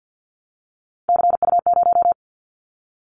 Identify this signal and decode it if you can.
Morse code, heard as a keyed tone, about 50 wpm, 700 Hz. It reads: CU0
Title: Callsign Prefixes for Top 50 DXCC - 50wpm